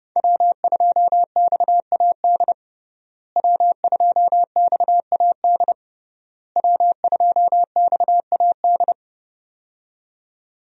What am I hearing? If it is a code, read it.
Morse code, heard as a keyed tone, about 30 wpm, 700 Hz. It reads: W2XAD W2XAD W2XAD
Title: USA Call Signs - 5 Characters - Repeated 3x - 30wpm